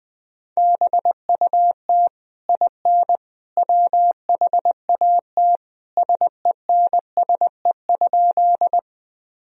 Morse code, heard as a keyed tone, about 21 wpm, 700 Hz. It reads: BUT IN WHAT SENSE?
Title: Sentences from Top 1000 Words - Review Part 4 - 20wpm